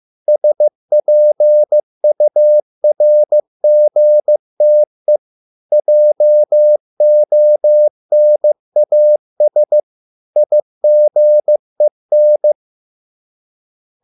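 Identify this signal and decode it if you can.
Morse code, heard as a keyed tone, about 15 wpm, 600 Hz. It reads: SPURGTE JONAS IGEN
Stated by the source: Morse code at a slow 15 wpm from CW med Gnister.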